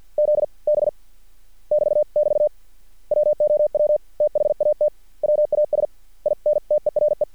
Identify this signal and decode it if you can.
Morse code, heard as a keyed tone, about 50 wpm, 600 Hz. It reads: Z B <BT><BT> WOW THAT WAS INTERE
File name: archived / CRSnet